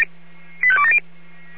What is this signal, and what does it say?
Five-tone selective call (ZVEI), heard as quick runs of five tones, tones of about 70 ms. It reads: 85270
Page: /496/SDR/ZVEI/85830/